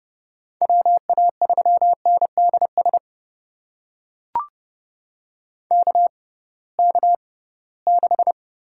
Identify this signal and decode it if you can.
Morse code, heard as a keyed tone, about 30 wpm, spaced as for 19 wpm, 700 Hz. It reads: WA3NDS KK6
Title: USA Call Signs - 6x Character Spacing - 30wpm